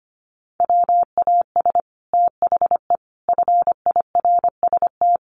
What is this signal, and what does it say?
Morse code, heard as a keyed tone, about 25 wpm, 700 Hz. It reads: WAS THE FIRST
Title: Sets of 3 Words 25wpm